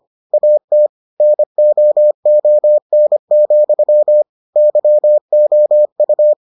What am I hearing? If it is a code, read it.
Morse code, heard as a keyed tone, about 25 wpm, 600 Hz. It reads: AT NOON, YOU